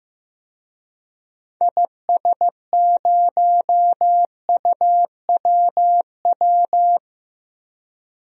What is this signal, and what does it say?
Morse code, heard as a keyed tone, about 15 wpm, 700 Hz. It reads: IS0UWW
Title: DXCC-100 Call Signs - 6 Characters - 15wpm